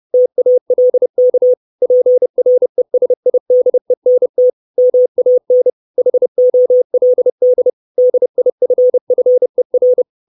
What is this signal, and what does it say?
Morse code, heard as a keyed tone, about 30 wpm, 500 Hz. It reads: TALK PRESIDENT MAN HOLD DIFFER
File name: 632 Words 30WPM